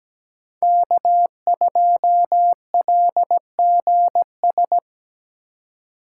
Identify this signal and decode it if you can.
Morse code, heard as a keyed tone, about 20 wpm, 700 Hz. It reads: K2LGS
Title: Speed Racing - USA 1x3 Call Signs 17wpm